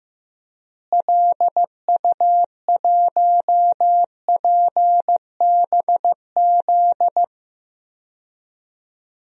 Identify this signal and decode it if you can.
Morse code, heard as a keyed tone, about 15 wpm, 700 Hz. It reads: LU1PBZ
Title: DXCC-100 Call Signs - 6 Characters - Repeated 2x - 15wpm